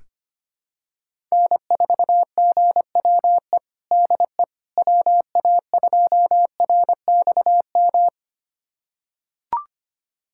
Morse code, heard as a keyed tone, about 25 wpm, 700 Hz. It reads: N4GWE DE WA2RXM
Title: USA Callsign DE Callsign 25wpm